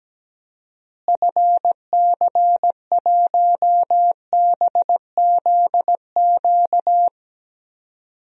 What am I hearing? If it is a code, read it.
Morse code, heard as a keyed tone, about 17 wpm, 700 Hz. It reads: FC1BZQ
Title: DXCC-100 Call Signs - 6 Characters - Repeated 2x - 17wpm